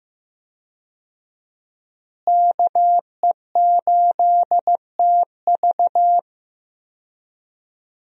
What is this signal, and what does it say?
Morse code, heard as a keyed tone, about 16 wpm, 700 Hz. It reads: KE8TV
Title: USA Call Signs - 5 Characters - Repeated 2x - 15wpm